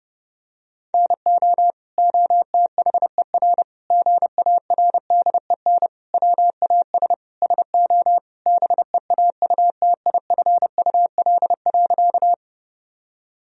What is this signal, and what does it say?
Morse code, heard as a keyed tone, about 30 wpm, 700 Hz. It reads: NO OTHER GARDEN WAS SO BEAUTIFUL.
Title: Sentences from Top 1000 Words - Review Part 4 - 30wpm